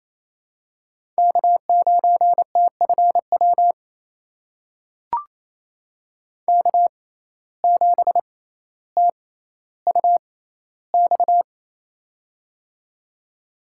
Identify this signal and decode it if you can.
Morse code, heard as a keyed tone, about 28 wpm, spaced as for 14 wpm, 700 Hz. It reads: K9TFW K7TUX
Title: USA Call Signs - 5 Characters - 6x Character Spacing - 28wpm